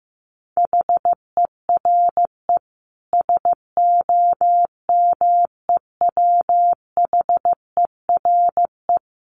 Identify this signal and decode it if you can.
Morse code, heard as a keyed tone, about 15 wpm, 700 Hz. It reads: HERE SOMEWHERE
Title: Sentences from Top 1400 Words 15wpm